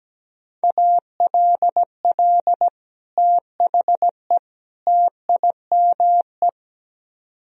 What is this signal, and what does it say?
Morse code, heard as a keyed tone, about 17 wpm, 700 Hz. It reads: ALL THE TIME
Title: Speed-Racing - Sets of 3 Words 17wpm